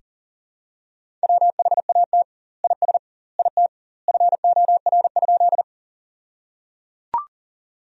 Morse code, heard as a keyed tone, about 37 wpm, 700 Hz. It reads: WHAT IS IT FOR?
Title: 2x Word-Spacing - Sentences from Top 100 Words - 40wpm